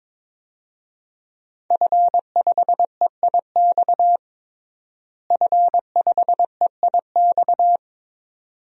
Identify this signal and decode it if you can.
Morse code, heard as a keyed tone, about 21 wpm, 700 Hz. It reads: F5EIX F5EIX
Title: DXCC-100 Call Signs - 5 Characters - Repeated 2x - 22wpm